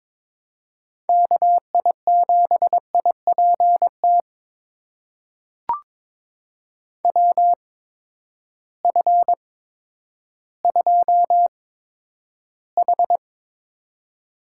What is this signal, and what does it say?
Morse code, heard as a keyed tone, about 22 wpm, spaced as for 11 wpm, 700 Hz. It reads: KI7IPT WF2H
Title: USA Call Signs - 8x Character Spacing - 22wpm